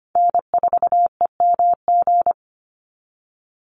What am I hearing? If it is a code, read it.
Morse code, heard as a keyed tone, about 25 wpm, 700 Hz. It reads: N4EMG
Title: NA SOTA Chaser Callsigns 25wpm